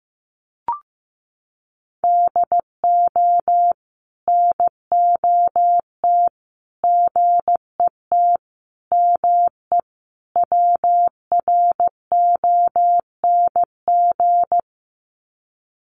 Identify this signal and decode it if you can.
Morse code, heard as a keyed tone, about 15 wpm, 700 Hz. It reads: DO NOT GET ME WRONG
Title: Sets of 5 Words Encore 15wpm